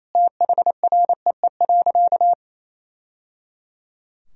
Morse code, heard as a keyed tone, about 28 wpm, 700 Hz. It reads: THREE.
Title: Sentences from Top 1000 Words - Review Part 1 - 28wpm